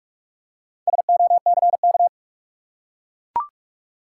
Morse code, heard as a keyed tone, about 44 wpm, 700 Hz. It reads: SOCK
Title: Four Letter Words - 45wpm